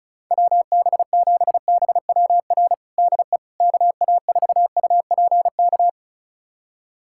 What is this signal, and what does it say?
Morse code, heard as a keyed tone, about 34 wpm, 700 Hz. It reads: WB7BWR DE KA4UPK
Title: USA Callsign DE Callsign 35wpm